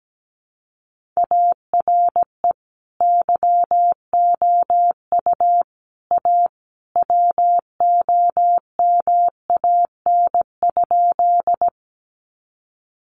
Morse code, heard as a keyed tone, about 17 wpm, 700 Hz. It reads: ARE YOU A WOMAN?
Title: Sentences from Top 1100 Words 17wpm